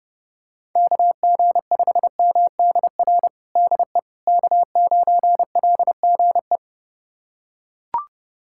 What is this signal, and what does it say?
Morse code, heard as a keyed tone, about 30 wpm, 700 Hz. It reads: KG5MDR DE K9LGE
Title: USA Callsign DE Callsign 30wpm